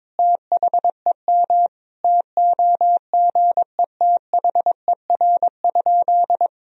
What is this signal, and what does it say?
Morse code, heard as a keyed tone, about 22 wpm, 700 Hz. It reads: THEM TOGETHER?
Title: Sentences from Top 1000 Words - Review Part 1 - 22wpm